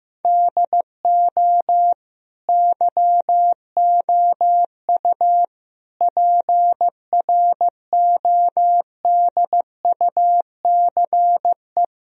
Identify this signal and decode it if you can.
Morse code, heard as a keyed tone, about 15 wpm, 700 Hz. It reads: DO YOU PRODUCE
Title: Sentences from Top 1000 Words - Review Part 4 - 15wpm